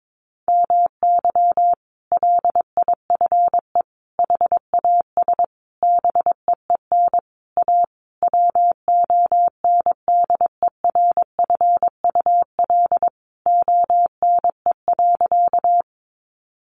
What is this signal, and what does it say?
Morse code, heard as a keyed tone, about 22 wpm, 700 Hz. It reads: MY LIFE HAS BEEN A WONDERFUL ONE.
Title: Sentences from Top 1100 Words 22wpm